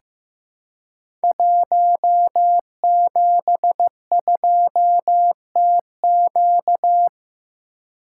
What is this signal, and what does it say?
Morse code, heard as a keyed tone, about 15 wpm, 700 Hz. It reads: 172TQ
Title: Code Groups - 5 Letter:Numbers - 15wpm